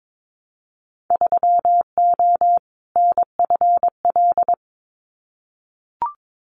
Morse code, heard as a keyed tone, about 22 wpm, 700 Hz. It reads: 3O NFL
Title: Winter Field Day Exchanges 22wpm